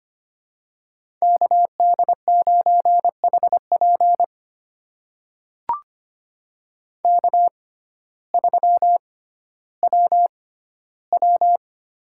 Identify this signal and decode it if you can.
Morse code, heard as a keyed tone, about 25 wpm, spaced as for 14 wpm, 700 Hz. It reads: KD9HP K3WW
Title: USA Call Signs - 6x Character Spacing - 25wpm